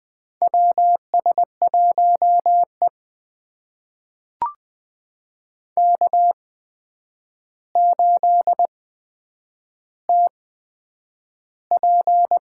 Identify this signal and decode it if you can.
Morse code, heard as a keyed tone, about 20 wpm, spaced as for 9 wpm, 700 Hz. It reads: WS1E K8TP
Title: USA Call Signs - 4 Characters - 8x Character Spacing - 20wpm